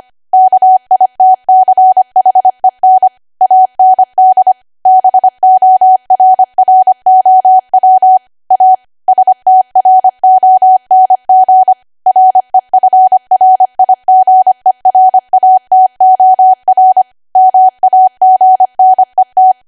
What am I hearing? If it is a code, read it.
Morse code, heard as a keyed tone, about 25 wpm, 750 Hz. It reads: KITCHEN AND BORROW A STRONG REFRIGERATOR MAGNET